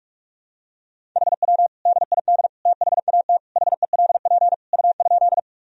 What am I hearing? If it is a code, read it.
Morse code, heard as a keyed tone, about 47 wpm, 700 Hz. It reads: HW DID THAT HELP U?
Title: Sentences from Top 200 Words and Abbreviations 45wpm